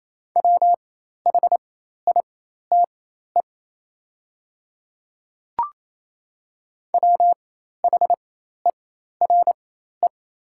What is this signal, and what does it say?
Morse code, heard as a keyed tone, about 28 wpm, spaced as for 13 wpm, 700 Hz. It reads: WHITE WHERE
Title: 4x Character Spacing - Top 500 Words - 28wpm